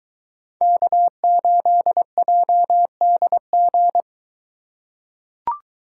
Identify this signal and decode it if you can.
Morse code, heard as a keyed tone, about 22 wpm, 700 Hz. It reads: K8JDG